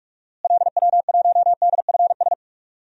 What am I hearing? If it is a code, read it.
Morse code, heard as a keyed tone, about 46 wpm, 700 Hz. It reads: LW1DFS
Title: DXCC-100 Call Signs - 6 Characters - Repeated 2x - 45wpm